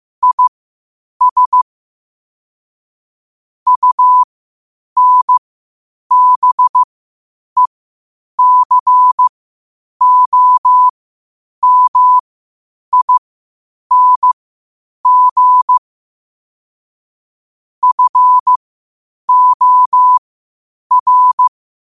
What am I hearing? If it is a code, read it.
Morse code, heard as a keyed tone, about 15 wpm, spaced as for 8 wpm, 1000 Hz. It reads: IS UNBECOMING FOR
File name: qotdfiles / 8wpm